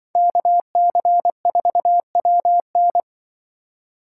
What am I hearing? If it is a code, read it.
Morse code, heard as a keyed tone, about 24 wpm, 700 Hz. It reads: KC4WN
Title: Speed Racing - USA Call Signs 20wpm